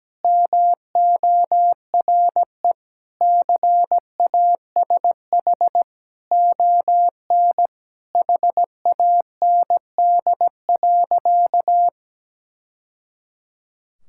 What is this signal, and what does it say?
Morse code, heard as a keyed tone, about 17 wpm, 700 Hz. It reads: MORE CASH ON HAND.
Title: Sentences from Top 1000 Words - Review Part 5 - 17wpm